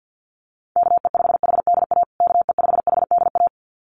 Morse code, heard as a keyed tone, about 50 wpm, 700 Hz. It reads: KE5HDA KE5HDA
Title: USA Call Signs Repeated 2x 50wpm